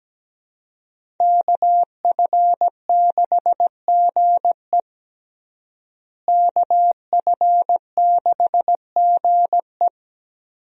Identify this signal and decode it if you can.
Morse code, heard as a keyed tone, about 17 wpm, 700 Hz. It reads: KF6GE KF6GE
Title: USA Call Signs - 5 Characters - Repeated 2x - 17wpm